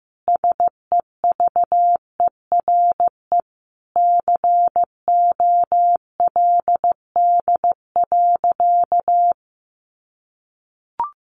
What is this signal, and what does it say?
Morse code, heard as a keyed tone, about 15 wpm, 700 Hz. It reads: SEVERE COLD.
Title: Sentences from Top 1400 Words 15wpm